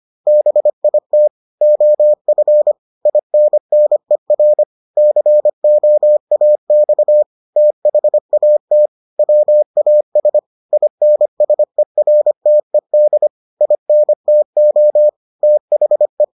Morse code, heard as a keyed tone, about 25 wpm, 600 Hz. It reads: BIT OF INNER COAX THAT WAS INSERTED INTO THE